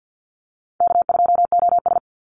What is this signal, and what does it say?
Morse code, heard as a keyed tone, about 50 wpm, 700 Hz. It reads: K2OS
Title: Easy Call Signs Repeated 3x - 50wpm